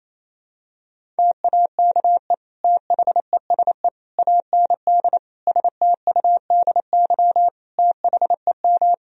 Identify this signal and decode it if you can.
Morse code, heard as a keyed tone, about 28 wpm, 700 Hz. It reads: TAKE THESE AND STUDY THEM
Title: Sentences from Top 1000 Words - Review Part 1 - 28wpm